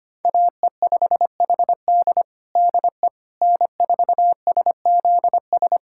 Morse code, heard as a keyed tone, about 25 wpm, 700 Hz. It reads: AE5HD DE N4SZS
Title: USA Callsign DE Callsign - 5 Characters - 25wpm